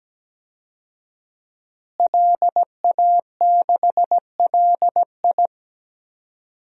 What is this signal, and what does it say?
Morse code, heard as a keyed tone, about 17 wpm, 700 Hz. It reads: LA6LI
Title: DXCC-100 Call Signs - 5 Characters - Repeated 2x - 17wpm